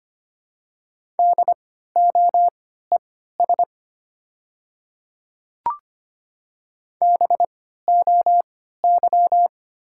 Morse code, heard as a keyed tone, about 25 wpm, spaced as for 14 wpm, 700 Hz. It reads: DOES BOY
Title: 3x Character Spacing - Top 200 Words - 25wpm